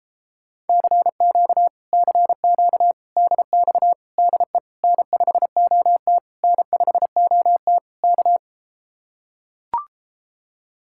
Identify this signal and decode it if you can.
Morse code, heard as a keyed tone, about 33 wpm, 700 Hz. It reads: CQ CQ DX DE N5OT N5OT K